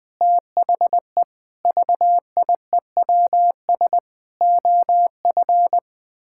Morse code, heard as a keyed tone, about 21 wpm, 700 Hz. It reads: THE VIEWS OF
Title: Sentences with 8-Word Phrases - Spoken Beginning - 20wpm